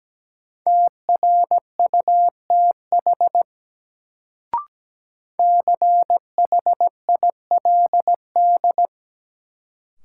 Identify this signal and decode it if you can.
Morse code, heard as a keyed tone, about 17 wpm, 700 Hz. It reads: TRUTH CHILD
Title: Sentences from Top 800 Words - New Word Drill - 17wpm